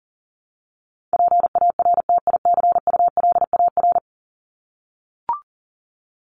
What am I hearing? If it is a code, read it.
Morse code, heard as a keyed tone, about 38 wpm, 700 Hz. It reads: PARTICULAR
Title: LY Suffix - Top 100 Words with Base Words 40wpm